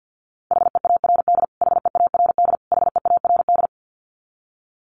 Morse code, heard as a keyed tone, about 50 wpm, 700 Hz. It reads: HEARD HEARD HEARD
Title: Top 500 Words Repeated 50wpm